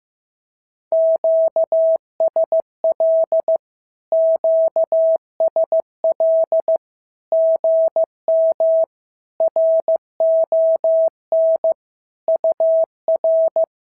Morse code, heard as a keyed tone, about 15 wpm, 650 Hz. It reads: QSL QSL GM RON UR
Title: SOTA QSO Practice 15wpm